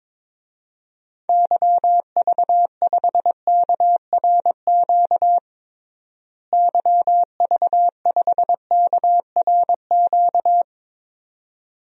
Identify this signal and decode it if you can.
Morse code, heard as a keyed tone, about 22 wpm, 700 Hz. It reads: YV5KRQ YV5KRQ
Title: DXCC-100 Call Signs - 6 Characters - Repeated 2x - 22wpm